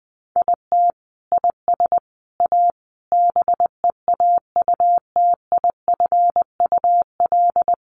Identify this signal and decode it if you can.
Morse code, heard as a keyed tone, about 20 wpm, 700 Hz. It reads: IT IS A BEAUTIFUL
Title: Sentences from Top 1200 Words 20wpm